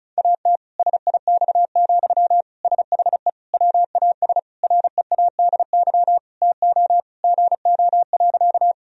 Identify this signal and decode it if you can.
Morse code, heard as a keyed tone, about 34 wpm, 700 Hz. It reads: AT SIX, SHE WAS READY TO GO.
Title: Sentences from Top 1000 Words - Review Part 2 - 35wpm